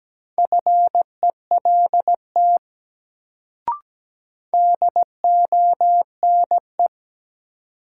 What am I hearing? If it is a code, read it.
Morse code, heard as a keyed tone, about 17 wpm, 700 Hz. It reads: FELT DONE
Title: Sets of 5 Words Encore - New Word Drill - 17wpm